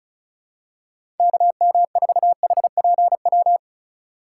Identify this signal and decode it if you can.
Morse code, heard as a keyed tone, about 36 wpm, 700 Hz. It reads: KM4HPW
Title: USA Call Signs - 6 Characters - 2x Character Spacing - 35wpm